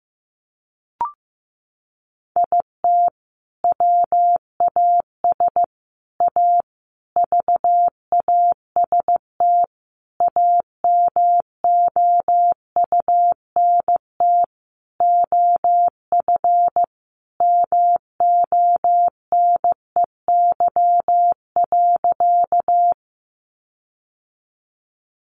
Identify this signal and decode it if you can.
Morse code, heard as a keyed tone, about 14 wpm, 700 Hz. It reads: IT WAS A VAST AMOUNT OF MONEY.
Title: Sentences from Top 1300 Words 15wpm